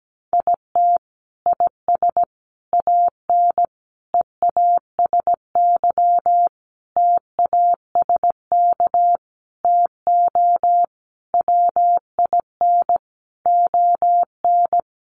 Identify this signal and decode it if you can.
Morse code, heard as a keyed tone, about 17 wpm, 700 Hz. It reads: IT IS AN EASY TASK TO WIN ON
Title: Sentences from Top 1500 Words 17wpm